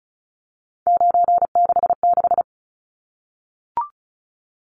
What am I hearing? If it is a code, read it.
Morse code, heard as a keyed tone, about 34 wpm, 700 Hz. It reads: 966
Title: Numbers - 3-Digit 35wpm